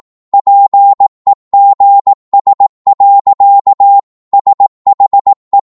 Morse code, heard as a keyed tone, about 18 wpm, 800 Hz. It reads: PEGS. SHE
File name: Morse / psg-18wpm